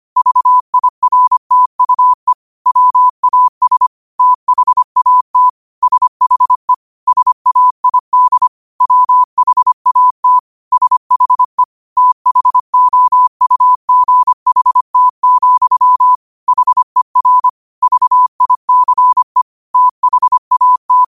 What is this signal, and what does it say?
Morse code, heard as a keyed tone, about 25 wpm, 1000 Hz. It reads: VIRTUE WAS THAT SHE SAID WHAT SHE THOUGHT, HER VICE THAT
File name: qotdfiles / 25wpm